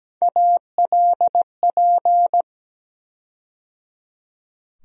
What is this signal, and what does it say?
Morse code, heard as a keyed tone, about 17 wpm, 700 Hz. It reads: ALP
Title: USA Call Sign Postfixes - 3 Letters 17wpm